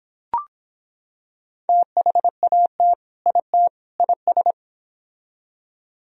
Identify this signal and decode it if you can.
Morse code, heard as a keyed tone, about 25 wpm, 700 Hz. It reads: THAT IT IS